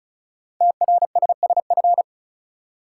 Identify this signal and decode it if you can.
Morse code, heard as a keyed tone, about 36 wpm, 700 Hz. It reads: TRSSF
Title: Code Groups - 5 Letters - 35wpm